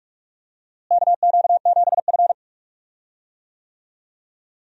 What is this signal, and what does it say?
Morse code, heard as a keyed tone, about 46 wpm, 700 Hz. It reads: KQ7F
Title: USA Call Signs - 4 Characters - Repeated 2x - 45wpm